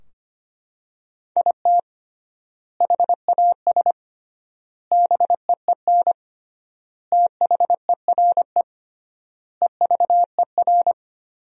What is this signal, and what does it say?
Morse code, heard as a keyed tone, about 25 wpm, 700 Hz. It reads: IT HAS BEEN THERE EVER
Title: 3x Word-Spacing - Sentences from Top 400 Words - 25wpm